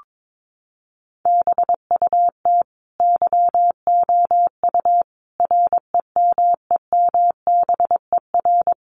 Morse code, heard as a keyed tone, about 22 wpm, 700 Hz. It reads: BUT YOU REMEMBER
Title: Sentences from Top 1300 Words 22wpm